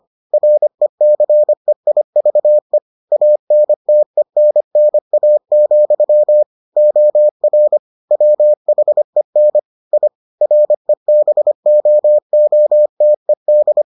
Morse code, heard as a keyed tone, about 25 wpm, 600 Hz. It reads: RECEIVE ANTENNA, OR WHEN I REBOOTED